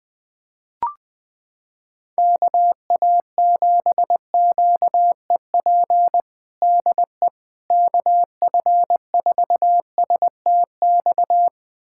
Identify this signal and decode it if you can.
Morse code, heard as a keyed tone, about 21 wpm, 700 Hz. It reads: KA7QEP DE KF4STX
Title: USA Callsign DE Callsign 20wpm